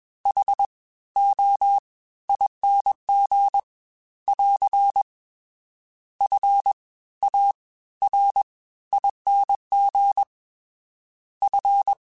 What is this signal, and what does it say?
Morse code, heard as a keyed tone, about 21 wpm, spaced as for 15 wpm, 800 Hz. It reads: HOING <AR> FARING F